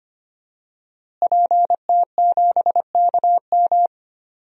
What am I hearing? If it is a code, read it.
Morse code, heard as a keyed tone, about 25 wpm, 700 Hz. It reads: PT7KM
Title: DXCC-100 Call Signs - 5 Characters - Repeated 2x - 25wpm